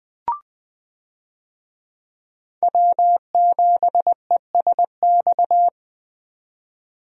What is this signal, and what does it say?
Morse code, heard as a keyed tone, about 21 wpm, 700 Hz. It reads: W7ESX
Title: USA Call Signs - 5 Characters - 20wpm